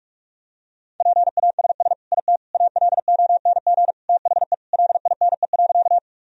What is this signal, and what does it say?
Morse code, heard as a keyed tone, about 46 wpm, 700 Hz. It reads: PASS IT ALONG THE LINE.
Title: Sentences from Top 1000 Words - Review Part 1 - 45wpm